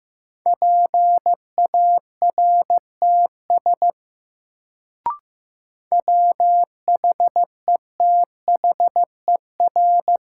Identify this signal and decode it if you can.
Morse code, heard as a keyed tone, about 15 wpm, 700 Hz. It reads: PARTS WHETHER
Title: Sets of 3 Words - New Word Drill - 15wpm